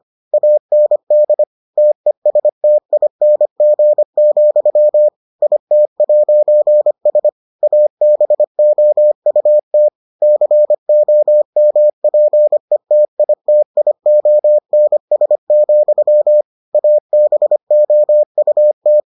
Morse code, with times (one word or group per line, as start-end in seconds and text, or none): 0.3-1.4 AND
1.8-5.1 TESTING,
5.4-7.3 IT'S
7.6-9.9 ABOUT
10.2-16.4 COMPETITIONS,
16.7-19.0 ABOUT